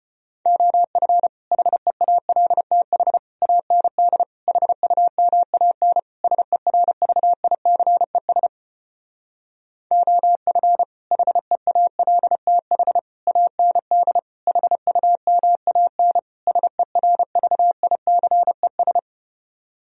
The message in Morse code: OF HEALTH AND HUMAN SERVICES OF HEALTH AND HUMAN SERVICES

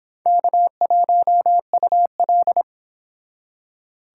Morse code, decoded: K1UL

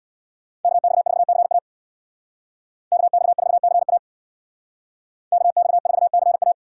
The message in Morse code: KX4CA KX4CA KX4CA

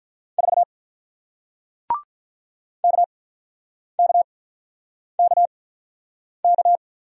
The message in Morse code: 4 KKKK

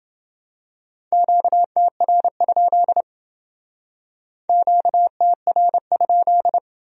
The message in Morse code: QTR? QTR?

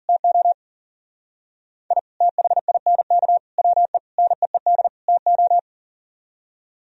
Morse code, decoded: TO I THINK WE NEED TO

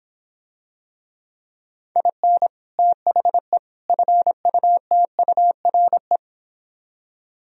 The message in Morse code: IN THE FUTURE

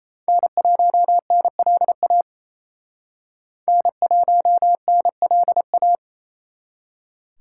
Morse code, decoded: N1NLA N1NLA

